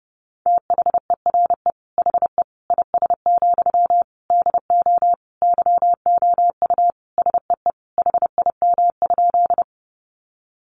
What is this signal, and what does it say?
Morse code, heard as a keyed tone, about 30 wpm, 700 Hz. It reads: THERE HE IS, DO YOU SEE HIM?
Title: Sentences from Top 100 Words 30wpm